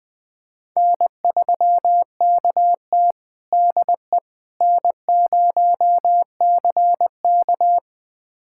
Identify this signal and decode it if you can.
Morse code, heard as a keyed tone, about 20 wpm, 700 Hz. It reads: N3KT DE N0CK
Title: USA Callsign DE Callsign - 4 Characters - 20wpm